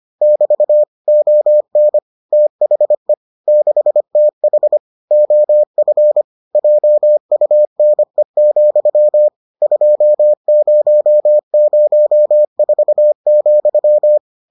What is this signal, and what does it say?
Morse code, heard as a keyed tone, about 25 wpm, 600 Hz. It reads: <BT> ON THE 6TH OF JUNE, 2004,